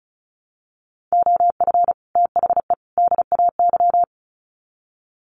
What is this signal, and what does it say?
Morse code, heard as a keyed tone, about 34 wpm, 700 Hz. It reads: OF THE DAY
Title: Sets of 3 Words 35wpm